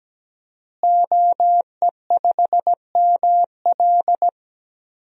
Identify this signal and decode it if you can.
Morse code, heard as a keyed tone, about 17 wpm, 700 Hz. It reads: OE5ML
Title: DXCC-100 Call Signs - 5 Characters - Repeated 3x - 17wpm